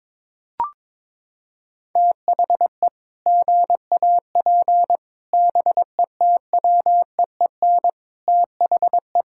Morse code, heard as a keyed tone, about 22 wpm, 700 Hz. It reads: THE GAP BETWEEN THE
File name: practice / 22wpm / Sets of 8 Words - 22wpm